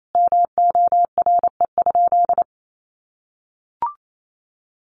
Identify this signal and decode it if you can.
Morse code, heard as a keyed tone, about 28 wpm, 700 Hz. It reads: MORE?
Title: Sentences from Top 1300 Words 28wpm